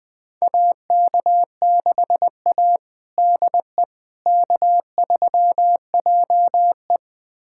Morse code, heard as a keyed tone, about 20 wpm, 700 Hz. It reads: AK6A DE K3JE
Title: USA Callsign DE Callsign - 4 Characters - 20wpm